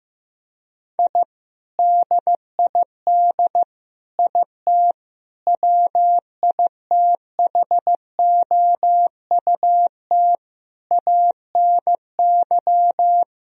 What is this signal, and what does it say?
Morse code, heard as a keyed tone, about 15 wpm, 700 Hz. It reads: I DID IT WITHOUT ANY
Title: Sentences from Top 1000 Words - Review Part 3 - 15wpm